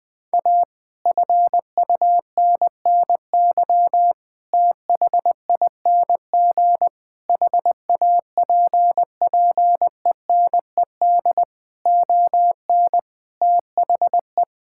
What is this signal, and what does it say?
Morse code, heard as a keyed tone, about 20 wpm, 700 Hz. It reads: A FUNNY THING HAPPENED ON THE
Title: Sentences with 9-Word Phrases - 20wpm